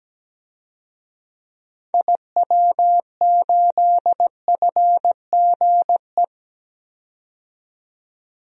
Morse code, heard as a keyed tone, about 17 wpm, 700 Hz. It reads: IW8FGE
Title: DXCC-100 Call Signs - 6 Characters - 17wpm